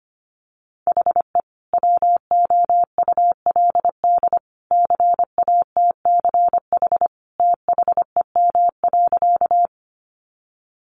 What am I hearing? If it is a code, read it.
Morse code, heard as a keyed tone, about 25 wpm, 700 Hz. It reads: HE WOULD CATCH THEM.
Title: Sentences from Top 1500 Words 25wpm